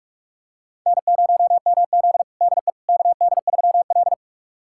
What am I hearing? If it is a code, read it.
Morse code, heard as a keyed tone, about 44 wpm, 690 Hz. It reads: N0MZ DE KD3L
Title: USA Callsign DE Callsign - 4 Characters - 45wpm